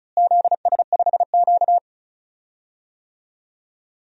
Morse code, heard as a keyed tone, about 36 wpm, 700 Hz. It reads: ZS5Q
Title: DXCC-100 Call Signs - 4 Characters - Repeated 2x - 35wpm